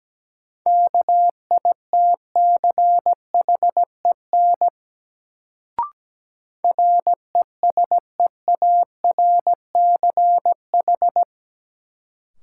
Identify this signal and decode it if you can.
Morse code, heard as a keyed tone, about 17 wpm, 700 Hz. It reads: KITCHEN RESEARCH
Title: Sentences from Top 1400 Words - New Word Drill Alt - 17wpm